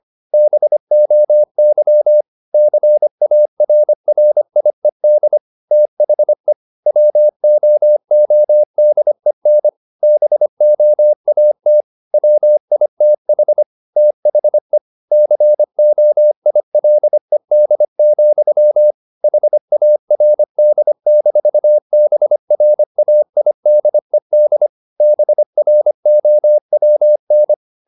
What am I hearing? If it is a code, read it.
Morse code, heard as a keyed tone, about 25 wpm, 600 Hz. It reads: BOY CARRIED THE WOODEN BOAT WITH THE COILED, HARD-BRAIDED BROWN
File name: the old man and the sea part 1 25wpm